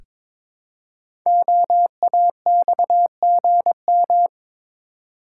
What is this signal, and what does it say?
Morse code, heard as a keyed tone, about 22 wpm, 700 Hz. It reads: OAXGM
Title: Code Groups - 5 Letters - 22wpm